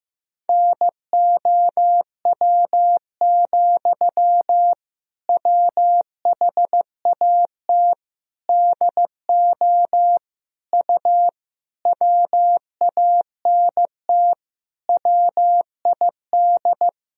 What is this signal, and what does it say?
Morse code, heard as a keyed tone, about 15 wpm, 700 Hz. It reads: NOW, WHAT DO U WANT WID
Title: Sentences from Top 200 Words and Abbreviations 15wpm